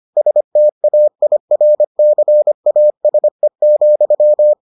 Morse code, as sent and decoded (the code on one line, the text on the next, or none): ... - .- .. .-. -.-. .- ... --..--
STAIRCASE,